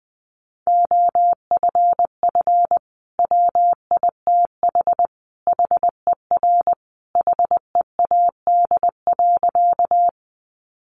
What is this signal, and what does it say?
Morse code, heard as a keyed tone, about 20 wpm, 700 Hz. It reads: OFF WITH HER HEAD.
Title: Sentences from Top 300 Words 20wpm